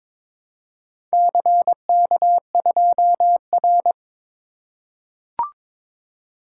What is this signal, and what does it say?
Morse code, heard as a keyed tone, about 22 wpm, 700 Hz. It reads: CK2R
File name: practice / 22wpm / Four Letter-Number Combinations - 22wpm